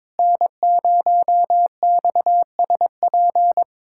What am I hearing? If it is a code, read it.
Morse code, heard as a keyed tone, about 22 wpm, 700 Hz. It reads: N0XSP
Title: USA Callsign DE Callsign - 5 Characters - 22wpm